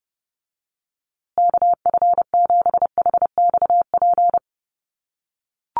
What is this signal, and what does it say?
Morse code, heard as a keyed tone, about 30 wpm, 700 Hz. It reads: KF7HXP
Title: USA Call Signs Repeated 3x 30wpm